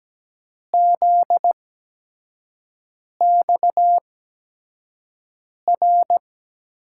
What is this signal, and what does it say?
Morse code, heard as a keyed tone, about 17 wpm, spaced as for 5 wpm, 700 Hz. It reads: ZXR
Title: USA Call Signs - 6 Characters - 8x Character Spacing - 17wpm